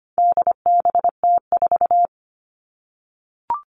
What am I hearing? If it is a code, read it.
Morse code, heard as a keyed tone, about 25 wpm, 700 Hz. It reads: DBT4
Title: All Cut Numbers - 4-Digit 25wpm